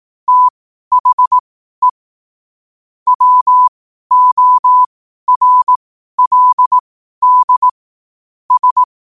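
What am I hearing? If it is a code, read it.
Morse code, heard as a keyed tone, about 18 wpm, spaced as for 12 wpm, 1000 Hz. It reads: THE WORLD S
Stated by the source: 12 wpm Morse.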